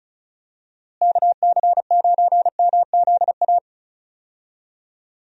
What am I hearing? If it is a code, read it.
Morse code, heard as a keyed tone, about 34 wpm, 700 Hz. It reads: KC9MZA
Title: USA Call Signs - 6 Characters - Repeated 3x - 35wpm